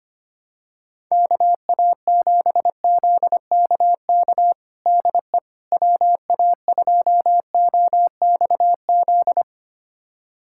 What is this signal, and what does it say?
Morse code, heard as a keyed tone, about 25 wpm, 700 Hz. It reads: KA7ZKK DE WA2OXZ
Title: USA Callsign DE Callsign - 6 Characters - 25wpm